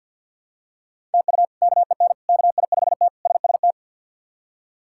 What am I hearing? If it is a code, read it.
Morse code, heard as a keyed tone, about 52 wpm, 700 Hz. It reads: TU KEN KI5T SST
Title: K1USN SST QSO Practice - 50wpm